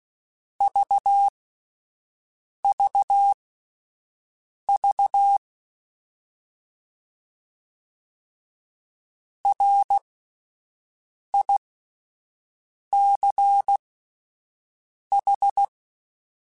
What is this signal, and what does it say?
Morse code, heard as a keyed tone, about 16 wpm, spaced as for 5 wpm, 800 Hz. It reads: VVV RICH